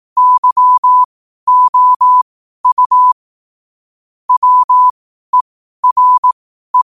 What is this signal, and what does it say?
Morse code, heard as a keyed tone, about 18 wpm, spaced as for 12 wpm, 1000 Hz. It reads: YOU WERE
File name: qotdfiles / 12wpm